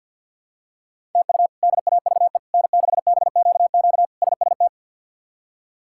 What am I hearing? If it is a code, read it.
Morse code, heard as a keyed tone, about 49 wpm, 700 Hz. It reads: TU DAVE N6BQQ SST
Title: K1USN SST QSO Practice - 50wpm